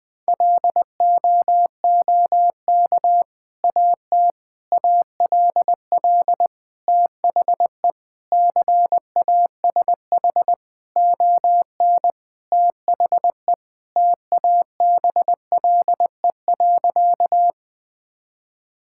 Morse code, heard as a keyed tone, about 20 wpm, 700 Hz. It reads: LOOK AT ALL THE CASH ON THE TABLE.
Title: Sentences from Top 1000 Words - Review Part 5 - 20wpm